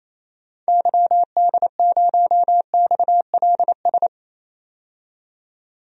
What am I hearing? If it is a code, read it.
Morse code, heard as a keyed tone, about 28 wpm, 700 Hz. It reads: YD0XLS
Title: DXCC-100 Call Signs - 6 Characters - 28wpm